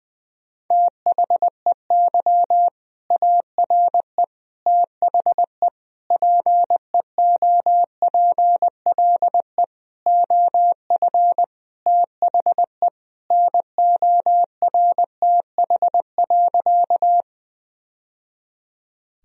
Morse code, heard as a keyed tone, about 20 wpm, 700 Hz. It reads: THEY ARE THE PEOPLE OF THE NORTH.
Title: Sentences from Top 1000 Words - Review Part 1 - 20wpm